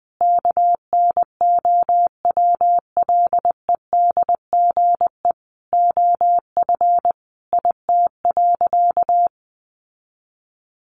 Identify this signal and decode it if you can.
Morse code, heard as a keyed tone, about 20 wpm, 700 Hz. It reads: KNOWLEDGE OF IT.
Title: Sentences from Top 1400 Words 20wpm